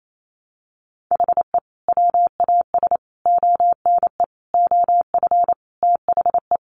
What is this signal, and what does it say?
Morse code, heard as a keyed tone, about 28 wpm, 700 Hz. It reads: HE WAS ONE OF THE
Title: Sets of 5 Words 28wpm